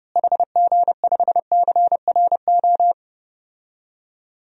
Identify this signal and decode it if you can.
Morse code, heard as a keyed tone, about 30 wpm, 700 Hz. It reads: HG5CRO
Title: DXCC-100 Call Signs - 6 Characters - Repeated 3x - 30wpm